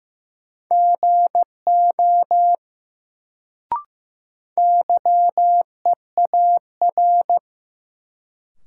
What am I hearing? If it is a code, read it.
Morse code, heard as a keyed tone, about 15 wpm, 700 Hz. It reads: GO YEAR
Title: Sets of 2 Words - New Word Drill - 15wpm